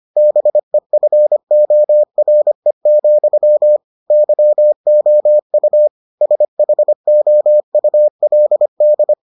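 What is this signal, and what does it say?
Morse code, heard as a keyed tone, about 25 wpm, 600 Hz. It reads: BEFORE, YOU SHOULD
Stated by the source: Morse edition of the Foundations of Amateur Radio episodes.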